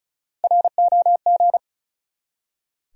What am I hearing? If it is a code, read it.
Morse code, heard as a keyed tone, about 36 wpm, 700 Hz. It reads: ROG
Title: USA Common Names - Max 4 Letters - 35wpm